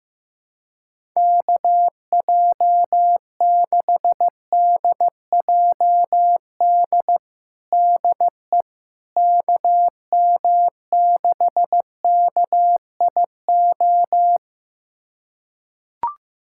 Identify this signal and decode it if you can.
Morse code, heard as a keyed tone, about 15 wpm, 700 Hz. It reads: KJ6DJD DE KM6KIO
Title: USA Callsign DE Callsign 15wpm